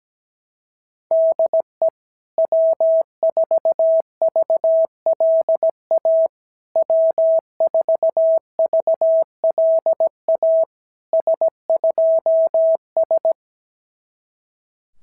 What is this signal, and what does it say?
Morse code, heard as a keyed tone, about 17 wpm, 650 Hz. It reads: DE W4VLA W4VLA S2S
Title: SOTA QSO Practice 17wpm